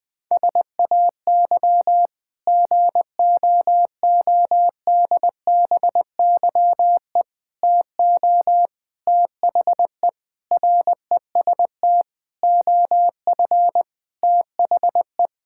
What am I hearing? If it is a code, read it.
Morse code, heard as a keyed tone, about 20 wpm, 700 Hz. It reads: SAY GOODBYE TO THE REST OF THE
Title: Sentences with 5-Word Phrases - 20wpm